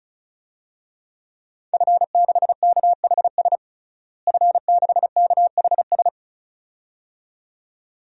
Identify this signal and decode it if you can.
Morse code, heard as a keyed tone, about 36 wpm, 700 Hz. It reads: F6KHS F6KHS
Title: DXCC-100 Call Signs - 5 Characters - Repeated 2x - 35wpm